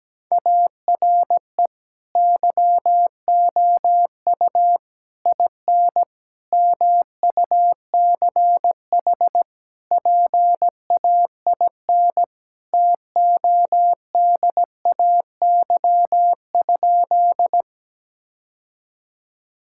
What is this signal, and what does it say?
Morse code, heard as a keyed tone, about 17 wpm, 700 Hz. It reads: ARE YOU IN MUCH PAIN TODAY?
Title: Sentences from Top 1600 Words 17wpm